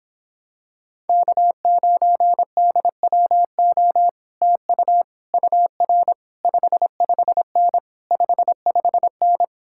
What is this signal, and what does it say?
Morse code, heard as a keyed tone, about 26 wpm, 700 Hz. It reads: K9DWO TU UR 55N 55N